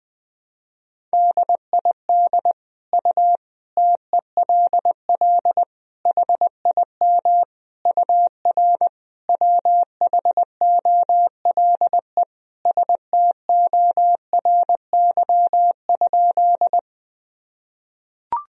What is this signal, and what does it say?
Morse code, heard as a keyed tone, about 21 wpm, 700 Hz. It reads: DID U TELL HIM UR WHOLE STORY?
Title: Sentences from Top 400 Words and Abbreviations 20wpm